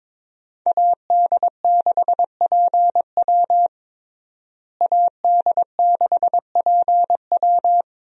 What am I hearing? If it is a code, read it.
Morse code, heard as a keyed tone, about 22 wpm, 700 Hz. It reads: AD6PW AD6PW
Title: USA Call Signs - 5 Characters - Repeated 2x - 22wpm